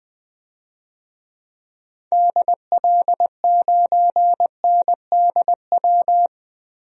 Morse code, heard as a keyed tone, about 20 wpm, 700 Hz. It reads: DL9NDW
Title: DXCC-100 Call Signs - 6 Characters - 20wpm